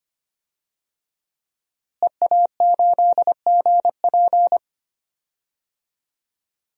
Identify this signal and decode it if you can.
Morse code, heard as a keyed tone, about 25 wpm, 700 Hz. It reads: EA8GP
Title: DXCC-100 Call Signs - 5 Characters - Repeated 2x - 25wpm